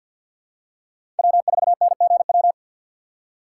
Morse code, heard as a keyed tone, about 51 wpm, 690 Hz. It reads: W4NGW
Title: USA Call Signs - 5 Characters - 6x Character Spacing - 50wpm